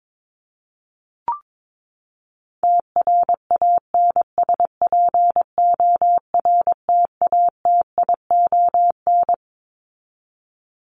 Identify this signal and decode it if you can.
Morse code, heard as a keyed tone, about 22 wpm, 700 Hz. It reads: TRANSPORTATION
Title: Top 1501-1750 Words 22wpm